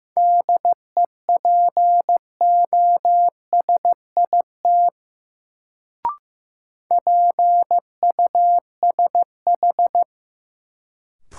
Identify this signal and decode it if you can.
Morse code, heard as a keyed tone, about 15 wpm, 700 Hz. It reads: DEPOSIT PUSH
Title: Sentences from Top 1300 Words - New Word Drill Alt - 15wpm